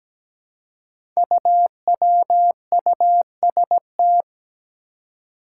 Morse code, heard as a keyed tone, about 17 wpm, 700 Hz. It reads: UWUST
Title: Five Challenging Letter-Number Combinations 17wpm